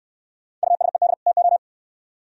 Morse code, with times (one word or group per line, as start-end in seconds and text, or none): 0.6-1.1 USED
1.3-1.6 TO